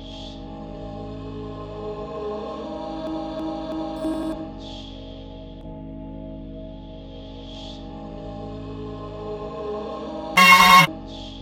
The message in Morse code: none